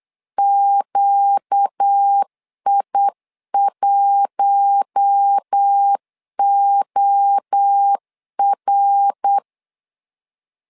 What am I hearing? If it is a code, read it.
Morse code, heard as a keyed tone, about 8 wpm, 800 Hz. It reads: QI1OR
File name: LEZIONI / 4 Q1 8wpm 3035